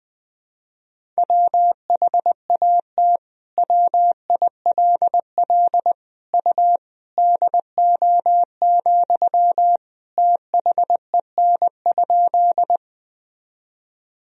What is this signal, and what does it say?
Morse code, heard as a keyed tone, about 20 wpm, 700 Hz. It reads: WHAT WILL U DO, THEN?
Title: Sentences from Top 100 Words and Abbreviations 20wpm